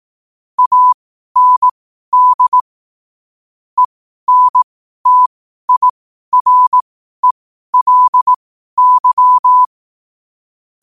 Morse code, heard as a keyed tone, about 18 wpm, spaced as for 12 wpm, 1000 Hz. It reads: AND ENTIRELY
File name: qotdfiles / 12wpm